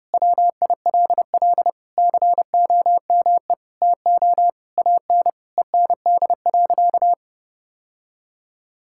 Morse code, heard as a keyed tone, about 32 wpm, 700 Hz. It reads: WILL COME TO AN END.
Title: Sentences from Top 1000 Words - Review Part 6 - 30wpm